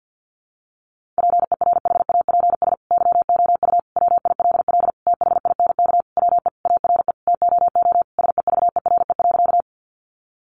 Morse code, heard as a keyed tone, about 50 wpm, 700 Hz. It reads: PERHAPS YOU WILL THINK WE ARE TOO SEVERE.
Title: Sentences from Top 1400 Words 50wpm